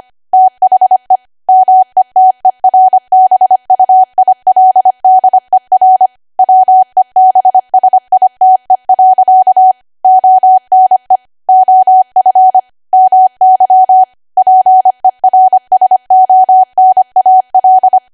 Morse code, heard as a keyed tone, about 25 wpm, 750 Hz. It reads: THE METERBUILDER WEBSITE. ONE OF MY PERSONAL
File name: cwa / W1AW-Level3 / 25WPM